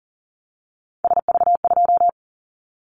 Morse code, heard as a keyed tone, about 40 wpm, 700 Hz. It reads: SV2
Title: Callsign Prefixes for Top 50 DXCC - 40wpm